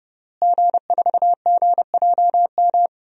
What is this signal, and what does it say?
Morse code, heard as a keyed tone, about 30 wpm, 700 Hz. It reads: G4GJM
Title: DXCC-100 Call Signs - 5 Characters - Repeated 2x - 30wpm